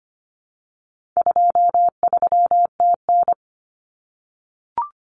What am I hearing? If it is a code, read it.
Morse code, heard as a keyed tone, about 25 wpm, 700 Hz. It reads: 23TN
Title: Common Cut Numbers - 4-Digit 25wpm